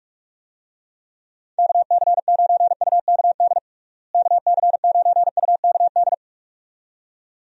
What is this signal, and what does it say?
Morse code, heard as a keyed tone, about 42 wpm, 700 Hz. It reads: KC9UKD KC9UKD
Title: USA Call Signs - 6 Characters - Repeated 2x - 45wpm